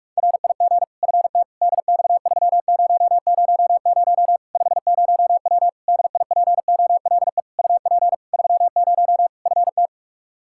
Morse code, heard as a keyed tone, about 46 wpm, 700 Hz. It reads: RIG FT DX3000 50W DIPOLE UP 30 FT
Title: Practice QSO - Exchange 2 - Info - 45wpm